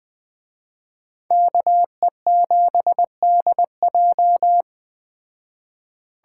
Morse code, heard as a keyed tone, about 20 wpm, 700 Hz. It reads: KE7DJ